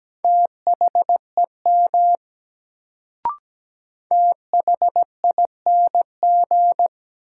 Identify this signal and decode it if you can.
Morse code, heard as a keyed tone, about 17 wpm, 700 Hz. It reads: THEM THING
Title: Sentences from Top 100 Words - New Word Drill - 17wpm